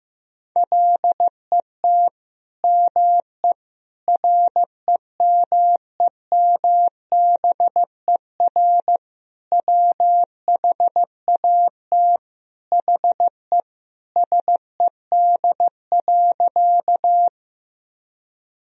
Morse code, heard as a keyed tone, about 15 wpm, 700 Hz. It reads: LET ME REMEMBER WHAT HE SED.
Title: Sentences from Top 400 Words and Abbreviations 15wpm